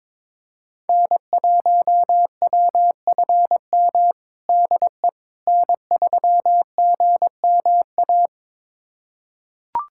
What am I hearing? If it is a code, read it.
Morse code, heard as a keyed tone, about 21 wpm, 700 Hz. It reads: N1WFM DE N3GMA
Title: USA Callsign DE Callsign - 5 Characters - 22wpm